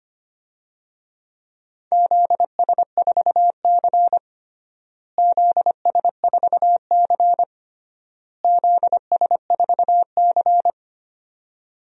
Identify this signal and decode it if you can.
Morse code, heard as a keyed tone, about 25 wpm, 700 Hz. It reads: ZS4C ZS4C ZS4C
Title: DXCC-100 Call Signs - 4 Characters - Repeated 3x - 25wpm